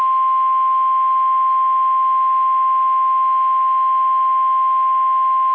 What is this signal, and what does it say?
Emergency Alert System (SAME) tones (the 1050 Hz attention tone). It is heard from 0.5 to 4.8 s.
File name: eas / KSDZKDJL / KSDZKDJL